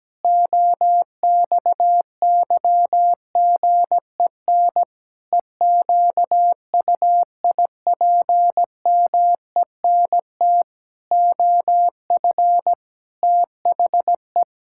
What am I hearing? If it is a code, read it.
Morse code, heard as a keyed tone, about 17 wpm, 700 Hz. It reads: OXYGEN EQUIPMENT OF THE
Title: Pangram Challenge v2 - 17wpm